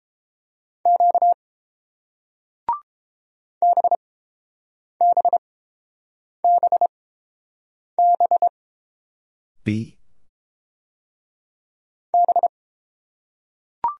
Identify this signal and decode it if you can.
Morse code, heard as a keyed tone, about 31 wpm, spaced as for 7 wpm, 700 Hz. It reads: Q BBBB B